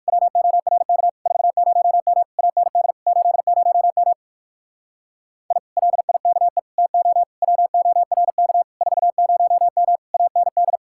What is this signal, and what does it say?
Morse code, heard as a keyed tone, about 59 wpm, 700 Hz. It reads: WORK 40M AND 80M I LIKE TO WORK 40M AND